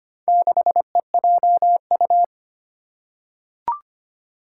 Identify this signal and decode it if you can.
Morse code, heard as a keyed tone, about 25 wpm, 700 Hz. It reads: 6EJU
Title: Four Challenging Letter-Number Combinations 25wpm